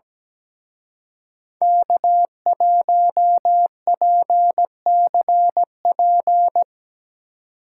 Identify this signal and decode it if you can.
Morse code, heard as a keyed tone, about 19 wpm, 700 Hz. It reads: K1PCP
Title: Speed Racing - USA Call Signs 17wpm